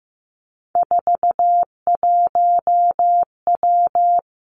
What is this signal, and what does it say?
Morse code, heard as a keyed tone, about 15 wpm, 700 Hz. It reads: 41W
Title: Three Challenging Letter-Number Combinations 15wpm